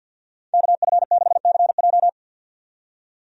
Morse code, heard as a keyed tone, about 53 wpm, 700 Hz. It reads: KF6CJ
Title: USA Call Signs - 5 Characters - 50wpm